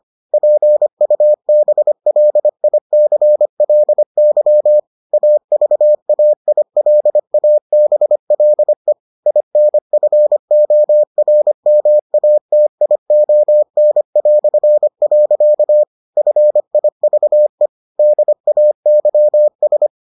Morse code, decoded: PUBLICLY AVAILABLE INFORMATION". FIVE DAYS

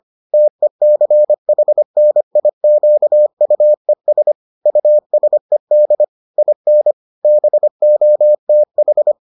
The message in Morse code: TECHNIQUES USED IN BOTH

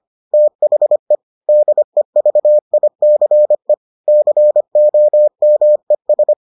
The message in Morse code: THE DEVICE COMES

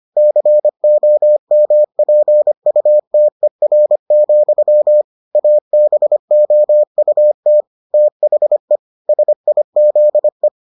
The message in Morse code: COMPUTER, ABOUT THE SIZE

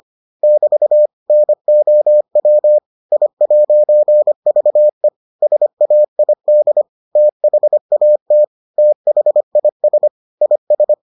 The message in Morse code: <BT> NOW I'VE SAID THAT THIS IS

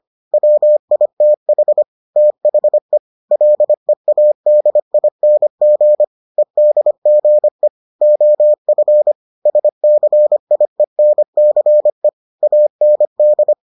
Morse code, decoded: WITH THE LEADING EDGE OF SCIENCE AND

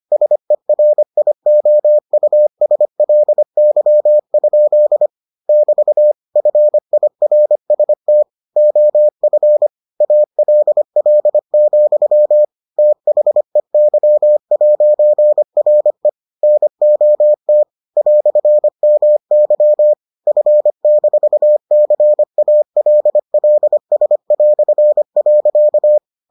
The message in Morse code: SERIOUSLY? <BT> FIRST OF ALL, THEY'RE NOT "MY F-CALLS".

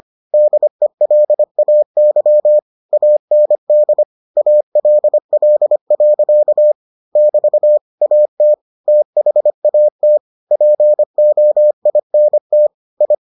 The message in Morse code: DELAY AND ALL. <BT> AT THAT POINT I

